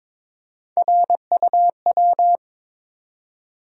Morse code, heard as a keyed tone, about 22 wpm, 700 Hz. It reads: RUW
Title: USA Call Sign Postfixes - 3 Letters 22wpm